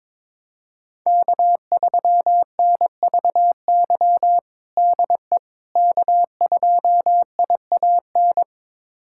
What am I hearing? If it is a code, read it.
Morse code, heard as a keyed tone, about 21 wpm, 700 Hz. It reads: K3NVY DE K2IAN
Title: USA Callsign DE Callsign - 5 Characters - 22wpm